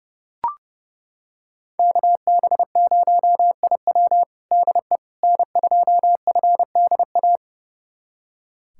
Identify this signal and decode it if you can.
Morse code, heard as a keyed tone, about 29 wpm, 700 Hz. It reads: KB0IW DE N2FDA
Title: USA Callsign DE Callsign 30wpm